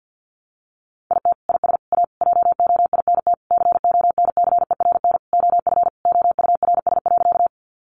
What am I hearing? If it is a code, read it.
Morse code, heard as a keyed tone, about 48 wpm, 700 Hz. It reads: IT IS A JOINT CONCERN OF OURS.
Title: Sentences from Top 1300 Words 50wpm